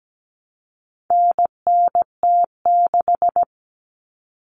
Morse code, heard as a keyed tone, about 17 wpm, 700 Hz. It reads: NNT6
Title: Common Cut Numbers - 4-Digit 17wpm